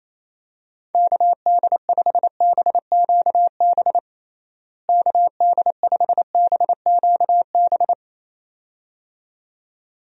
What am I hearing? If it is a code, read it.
Morse code, heard as a keyed tone, about 28 wpm, 700 Hz. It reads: KD5BQB KD5BQB
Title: USA Call Signs - 6 Characters - Repeated 2x - 28wpm